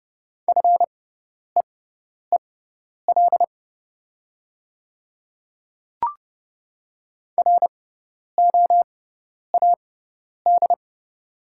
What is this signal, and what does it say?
Morse code, heard as a keyed tone, about 31 wpm, spaced as for 10 wpm, 700 Hz. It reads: FEEL ROAD
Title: 6x Character Spacing - Top 500 Words - 30wpm